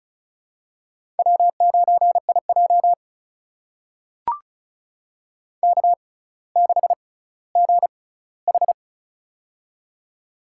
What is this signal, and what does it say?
Morse code, heard as a keyed tone, about 36 wpm, spaced as for 18 wpm, 700 Hz. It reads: W9IJ K6GH
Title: USA Call Signs - 6x Character Spacing - 35wpm